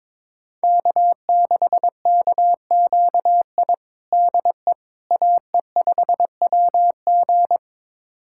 Morse code, heard as a keyed tone, about 21 wpm, 700 Hz. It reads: K6KQI DE AE5WG
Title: USA Callsign DE Callsign - 5 Characters - 22wpm